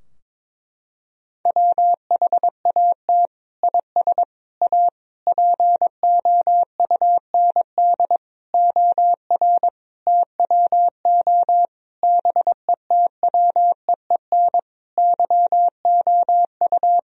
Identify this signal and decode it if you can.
Morse code, heard as a keyed tone, about 22 wpm, 700 Hz. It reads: WHAT IS A POUND OR TWO BETWEEN YOU
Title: Sentences from Top 1000 Words - Review Part 2 - 22wpm